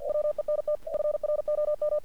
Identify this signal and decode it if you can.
Morse code, heard as a keyed tone, about 46 wpm, 600 Hz. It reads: WENT FROM
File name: archived / CRSnet